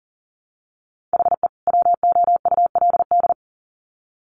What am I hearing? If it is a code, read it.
Morse code, heard as a keyed tone, about 40 wpm, 700 Hz. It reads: HE WOULD
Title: Sets of 2 Words 40wpm